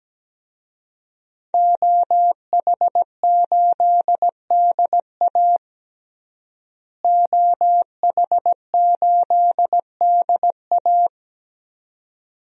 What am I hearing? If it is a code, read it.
Morse code, heard as a keyed tone, about 17 wpm, 700 Hz. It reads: OH8DA OH8DA
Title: DXCC-100 Call Signs - 5 Characters - Repeated 2x - 17wpm